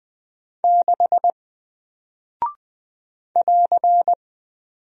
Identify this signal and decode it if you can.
Morse code, heard as a keyed tone, about 20 wpm, 700 Hz. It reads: 6 <AR>
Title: Single Character (Letter:Number:Punctuation:Prosign) Alt 20wpm